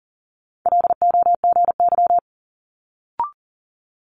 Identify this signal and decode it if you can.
Morse code, heard as a keyed tone, about 41 wpm, 700 Hz. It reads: LOGY
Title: Multi-letter Phonemes Quick Fire 40wpm